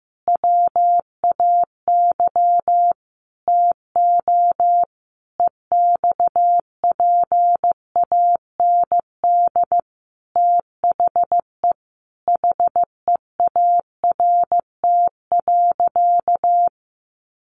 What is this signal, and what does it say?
Morse code, heard as a keyed tone, about 15 wpm, 700 Hz. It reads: WAY TO EXPAND THE HEART.
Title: Sentences from Top 1400 Words 15wpm